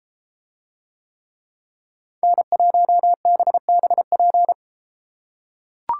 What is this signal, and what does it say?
Morse code, heard as a keyed tone, about 32 wpm, 700 Hz. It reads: N1BBP